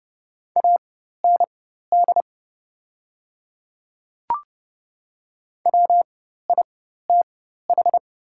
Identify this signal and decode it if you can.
Morse code, heard as a keyed tone, about 30 wpm, spaced as for 14 wpm, 700 Hz. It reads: AND WITH
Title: 4x Character Spacing - Top 200 Words - 30wpm